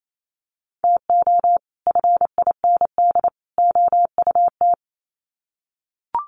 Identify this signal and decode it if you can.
Morse code, heard as a keyed tone, about 28 wpm, 700 Hz. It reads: TO FIND OUT
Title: Sets of 3 Words 28wpm